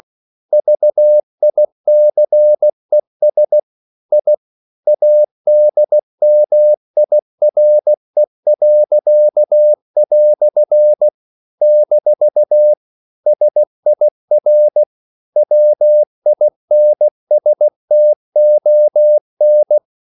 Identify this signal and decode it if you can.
Morse code, heard as a keyed tone, about 16 wpm, 600 Hz. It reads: VICES I ADMIRE." - SIR WINSTON
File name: CW / Practice / EnglishQuotations0416